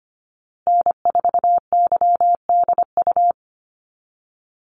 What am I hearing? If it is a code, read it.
Morse code, heard as a keyed tone, about 25 wpm, 700 Hz. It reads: N4YDU
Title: Call Signs Repeated 3x - 25wpm